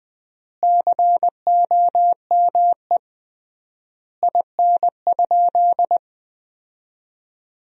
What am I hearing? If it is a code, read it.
Morse code, heard as a keyed tone, about 20 wpm, 700 Hz. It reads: COME IN?
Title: 3x Word-Spacing - Sentences from Top 100 Words - 20wpm